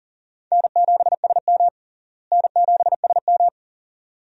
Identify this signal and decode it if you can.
Morse code, heard as a keyed tone, about 40 wpm, 700 Hz. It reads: N7SM N7SM
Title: USA Call Signs - 4 Characters - Repeated 2x - 40wpm